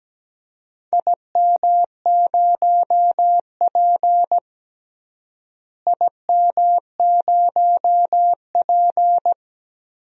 Morse code, heard as a keyed tone, about 17 wpm, 700 Hz. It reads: IM0P IM0P
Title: DXCC-100 Call Signs - 4 Characters - Repeated 3x - 17wpm